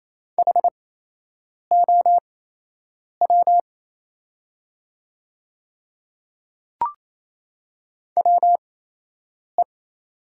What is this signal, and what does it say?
Morse code, heard as a keyed tone, about 27 wpm, spaced as for 8 wpm, 700 Hz. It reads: HOW WE
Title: 8x Character Spacing - Top 200 Words - 28wpm